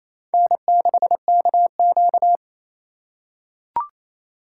Morse code, heard as a keyed tone, about 28 wpm, 700 Hz. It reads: N6KQ
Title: USA Call Signs - 2x Character Spacing - 28wpm